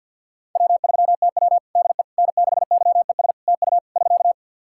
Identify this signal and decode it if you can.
Morse code, heard as a keyed tone, about 51 wpm, 700 Hz. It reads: W3TW DE N6YES TU <SK>
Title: Practice QSO - Exchange 4 - Ending - 50wpm